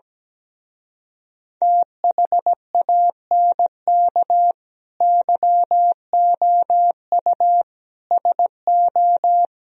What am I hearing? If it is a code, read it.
Morse code, heard as a keyed tone, about 17 wpm, 700 Hz. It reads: THANK YOU SO
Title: Speed-Racing - Sets of 5 Words 17wpm